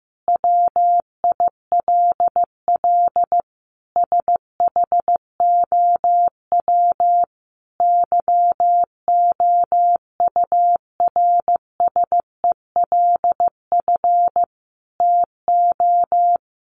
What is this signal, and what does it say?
Morse code, heard as a keyed tone, about 15 wpm, 700 Hz. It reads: WILL SHOW YOURSELF TO
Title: Sentences from Top 600 Words 15wpm